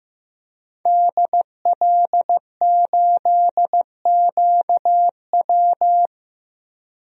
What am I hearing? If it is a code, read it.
Morse code, heard as a keyed tone, about 15 wpm, 700 Hz. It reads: DL8QW
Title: DXCC-100 Call Signs - 5 Characters - 15wpm